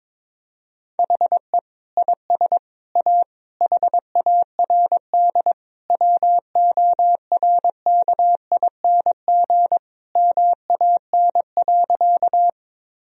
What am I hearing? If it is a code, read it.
Morse code, heard as a keyed tone, about 22 wpm, 700 Hz. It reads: HE IS A HARD WORKING MAN.
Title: Sentences from Top 1000 Words - Review Part 4 - 22wpm